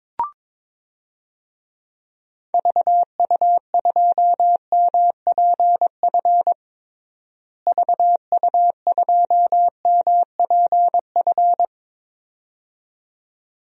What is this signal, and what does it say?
Morse code, heard as a keyed tone, about 22 wpm, 700 Hz. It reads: VU2MPF VU2MPF
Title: DXCC-100 Call Signs - 6 Characters - Repeated 2x - 22wpm